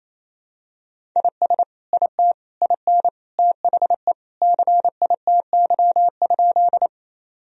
Morse code, heard as a keyed tone, about 27 wpm, 700 Hz. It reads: IS IT IN THE CITY?
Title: Sentences from Top 1000 Words - Review Part 1 - 28wpm